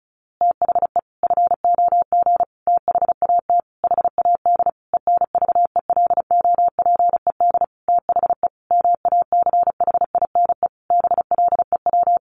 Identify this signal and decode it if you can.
Morse code, heard as a keyed tone, about 34 wpm, 700 Hz. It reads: THE FOG THAT HAD ENVELOPED THE MACHINE BLEW
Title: A Sound Of Thunder 35wpm